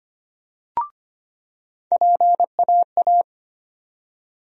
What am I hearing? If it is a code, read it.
Morse code, heard as a keyed tone, about 25 wpm, 700 Hz. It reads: PAA
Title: USA Call Sign Postfixes - 3 Letters 25wpm